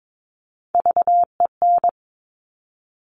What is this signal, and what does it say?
Morse code, heard as a keyed tone, about 22 wpm, 700 Hz. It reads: VEN
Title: Top Three Letter Combinations in Words 22wpm